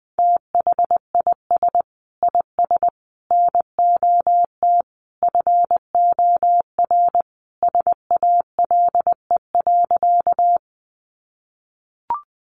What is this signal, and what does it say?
Morse code, heard as a keyed tone, about 20 wpm, 700 Hz. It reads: THIS IS NOT FOR SALE.
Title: Sentences from Top 1400 Words 20wpm